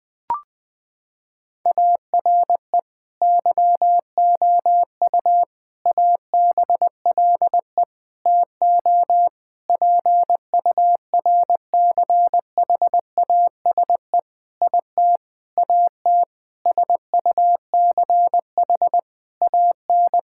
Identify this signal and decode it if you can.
Morse code, heard as a keyed tone, about 21 wpm, 700 Hz. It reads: ARE YOU ABLE TO PURCHASE IT AT SUCH AN
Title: Sentences from Top 1000 Words - Review Part 5 - 20wpm